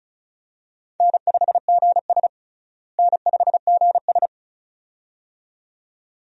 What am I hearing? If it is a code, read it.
Morse code, heard as a keyed tone, about 36 wpm, 700 Hz. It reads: N5GS N5GS
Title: USA Call Signs - 4 Characters - Repeated 2x - 35wpm